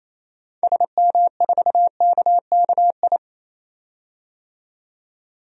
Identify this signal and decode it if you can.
Morse code, heard as a keyed tone, about 28 wpm, 700 Hz. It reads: SM4KKI
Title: DXCC-100 Call Signs - 6 Characters - 28wpm